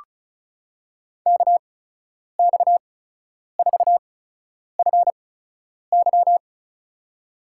Morse code, heard as a keyed tone, about 36 wpm, spaced as for 10 wpm, 700 Hz. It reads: KX4FY
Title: USA Call Signs - 8x Character Spacing - 35wpm